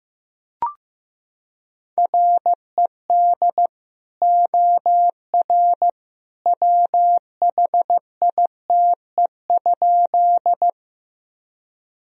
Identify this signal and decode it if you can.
Morse code, heard as a keyed tone, about 15 wpm, 700 Hz. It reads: RED OR WHITE?
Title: Sentences from Top 1000 Words - Review Part 2 - 15wpm